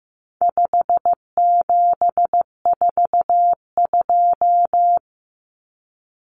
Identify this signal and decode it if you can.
Morse code, heard as a keyed tone, about 15 wpm, 700 Hz. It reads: 5742
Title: Common Cut Numbers - 4-Digit 15wpm